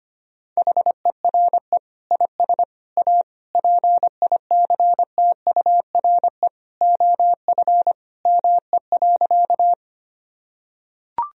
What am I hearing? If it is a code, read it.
Morse code, heard as a keyed tone, about 25 wpm, 700 Hz. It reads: HERE IS A PICTURE OF ME.
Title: Sentences from Top 1000 Words - Review Part 1 - 25wpm